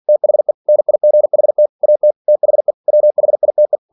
Morse code, heard as a keyed tone, about 46 wpm, 600 Hz. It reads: THE NIGHT AT THE WHITE
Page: Never 47wpm